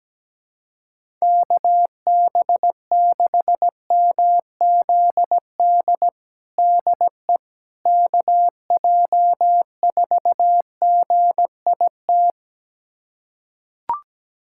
Morse code, heard as a keyed tone, about 17 wpm, 700 Hz. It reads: KB6MZD DE KJ4GIT
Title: USA Callsign DE Callsign - 6 Characters - 17wpm